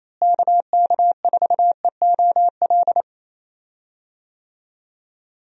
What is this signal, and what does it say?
Morse code, heard as a keyed tone, about 28 wpm, 700 Hz. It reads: KK4EOL
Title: USA Call Signs - 6 Characters - Repeated 3x - 28wpm